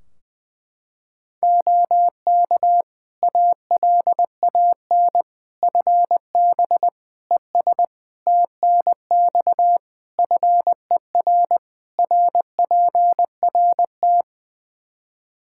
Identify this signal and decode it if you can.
Morse code, heard as a keyed tone, about 20 wpm, 700 Hz. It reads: OK ALAN FB ES TNX FER RPRT
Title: Practice QSO - Exchange 2 - Info - 20wpm